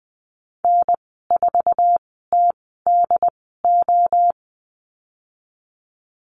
Morse code, heard as a keyed tone, about 20 wpm, spaced as for 14 wpm, 700 Hz. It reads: N4TDO
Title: 082 Lesson 14g - Number 2 - Call Signs 20wpm